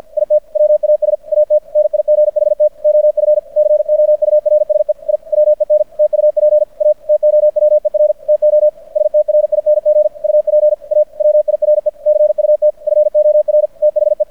Morse code, heard as a keyed tone, about 50 wpm, 600 Hz. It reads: IT WAS AT NIGHT OF COURSE I WENT UP A TOWER TO STRING UP A WIRE CUT FOR THE
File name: archived / CRSnet